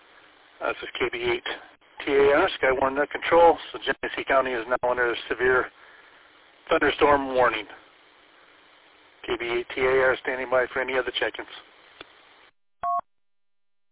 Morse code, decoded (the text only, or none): none